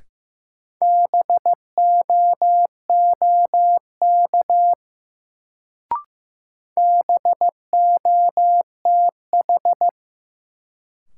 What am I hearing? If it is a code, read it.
Morse code, heard as a keyed tone, about 15 wpm, 700 Hz. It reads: BOOK BOTH
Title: Sentences from Top 300 Words - New Word Drill - 15wpm